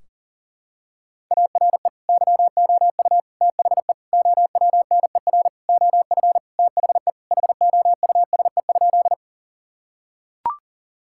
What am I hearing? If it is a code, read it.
Morse code, heard as a keyed tone, about 40 wpm, 700 Hz. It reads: ARE YOU THE OWNER OF THE HOUSE?
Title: Sentences from Top 1600 Words 40wpm